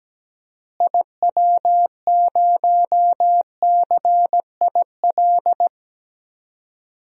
Morse code, heard as a keyed tone, about 17 wpm, 700 Hz. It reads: IW0CIL
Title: DXCC-100 Call Signs - 6 Characters - Repeated 2x - 17wpm